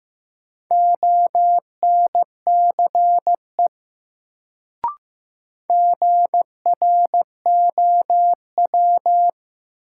Morse code, heard as a keyed tone, about 15 wpm, 700 Hz. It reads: ONCE GROW
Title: Sentences from Top 300 Words - New Word Drill - 15wpm